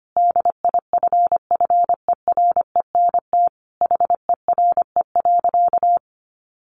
Morse code, heard as a keyed tone, about 25 wpm, 700 Hz. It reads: DIFFERENT HERE.
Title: Easy Phrases 25wpm